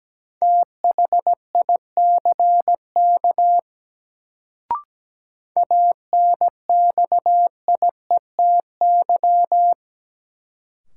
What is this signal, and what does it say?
Morse code, heard as a keyed tone, about 17 wpm, 700 Hz. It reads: THICK ANXIETY
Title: Sentences from Top 1600 Words - New Word Drill Alt - 17wpm